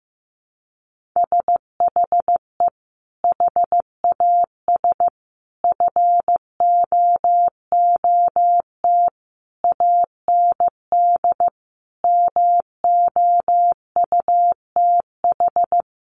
Morse code, heard as a keyed tone, about 15 wpm, 700 Hz. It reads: SHE HAS FOOT AND MOUTH
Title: Sentences from Top 1500 Words 15wpm